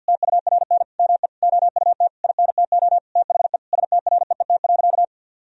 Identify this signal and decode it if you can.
Morse code, heard as a keyed tone, about 50 wpm, 700 Hz. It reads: TURN ME OUT INTO THE STREET.
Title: Sentences from Top 1000 Words - Review Part 4 - 50wpm